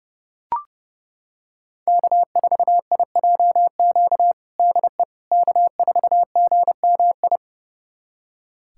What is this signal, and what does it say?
Morse code, heard as a keyed tone, about 30 wpm, 700 Hz. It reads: K4IJQ DE K4GMI
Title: USA Callsign DE Callsign 30wpm